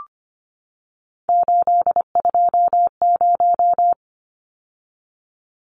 Morse code, heard as a keyed tone, about 25 wpm, 700 Hz. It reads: 820
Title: Numbers - 3-Digit 25wpm